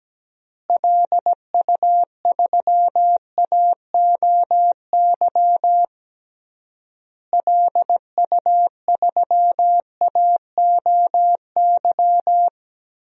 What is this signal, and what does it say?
Morse code, heard as a keyed tone, about 17 wpm, 700 Hz. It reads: LU3AOY LU3AOY
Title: DXCC-100 Call Signs - 6 Characters - Repeated 3x - 17wpm